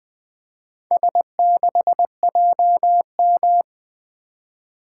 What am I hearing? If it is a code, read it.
Morse code, heard as a keyed tone, about 20 wpm, 700 Hz. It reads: S6JM
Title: Four Letter-Number Combinations - 20wpm